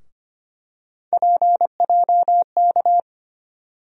Morse code, heard as a keyed tone, about 25 wpm, 700 Hz. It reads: PJK
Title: USA Call Sign Postfixes - 3 Letters 25wpm